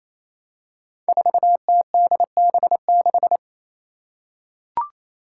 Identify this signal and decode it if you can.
Morse code, heard as a keyed tone, about 27 wpm, 700 Hz. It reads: 4TDB6
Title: Five Challenging Letter-Number Combinations 28wpm